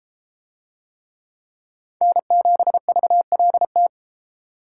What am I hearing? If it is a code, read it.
Morse code, heard as a keyed tone, about 31 wpm, 700 Hz. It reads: N7VLT